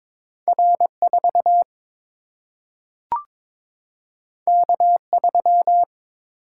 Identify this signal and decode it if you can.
Morse code, heard as a keyed tone, about 22 wpm, 700 Hz. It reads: R 4 K 3
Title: Single-Letter Single-Number 22wpm